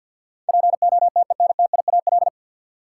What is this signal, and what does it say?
Morse code, heard as a keyed tone, about 47 wpm, 700 Hz. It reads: POTENTIAL
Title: Sentences from Top 1000 Words - Review Part 4 - New Word Drill - 50wpm